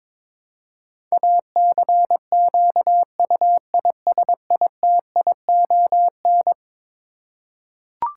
Morse code, heard as a keyed tone, about 22 wpm, 700 Hz. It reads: ACQUISITION
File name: practice / 22wpm / Eleven Letter Words - 22wpm